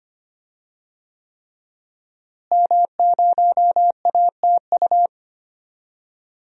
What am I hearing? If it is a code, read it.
Morse code, heard as a keyed tone, about 25 wpm, 700 Hz. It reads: M0ATU
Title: DXCC-100 Call Signs - 5 Characters - Repeated 3x - 25wpm